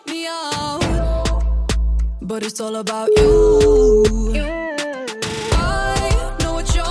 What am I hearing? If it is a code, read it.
Ringback tone (Europe, 425 Hz), with noise.